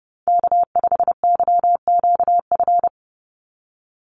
Morse code, heard as a keyed tone, about 30 wpm, 700 Hz. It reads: K5YQF
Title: Call Signs - 30wpm